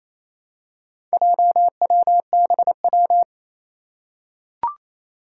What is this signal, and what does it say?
Morse code, heard as a keyed tone, about 28 wpm, 700 Hz. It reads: JWBW
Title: Four Challenging Letter-Number Combinations 28wpm